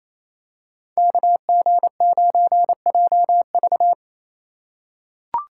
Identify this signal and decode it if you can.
Morse code, heard as a keyed tone, about 28 wpm, 700 Hz. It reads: KG9JV
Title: USA Call Signs - 5 Characters - 4x Character Spacing - 28wpm